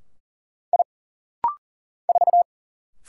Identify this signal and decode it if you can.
Morse code, heard as a keyed tone, about 40 wpm, 700 Hz. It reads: I 4
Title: Single Letter-Number - Rapid-Fire 40wpm